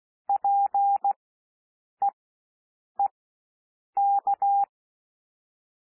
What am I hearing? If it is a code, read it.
Morse code, heard as a keyed tone, about 16 wpm, spaced as for 7 wpm, 800 Hz. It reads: PEEK
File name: cwmp3 / W1 SOUND WORDS 16 wpm